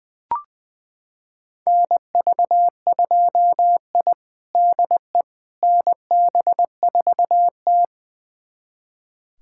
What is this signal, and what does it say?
Morse code, heard as a keyed tone, about 20 wpm, 700 Hz. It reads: NV2I DE NB4T
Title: USA Callsign DE Callsign - 4 Characters - 20wpm